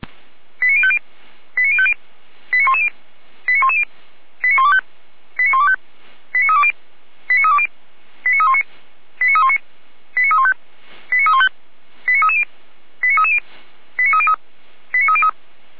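Five-tone selective call (ZVEI), heard as quick runs of five tones, tones of about 70 ms. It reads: 89050 89050 89110 89110 89125 89125 89320 89320 89319 89319 89315 89315 89330 89330 89303 89303
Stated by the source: ZVEI five-tone selective call 89042 received by SDR.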